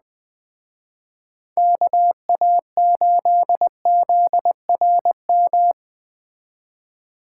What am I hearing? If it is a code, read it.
Morse code, heard as a keyed tone, about 20 wpm, 700 Hz. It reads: KA8ZRM
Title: Speed Racing - USA 2x3 Call Signs 17wpm